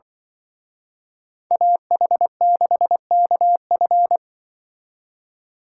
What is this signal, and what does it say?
Morse code, heard as a keyed tone, about 23 wpm, 700 Hz. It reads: AH6KF
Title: Speed Racing - USA 2x2 Call Signs 20wpm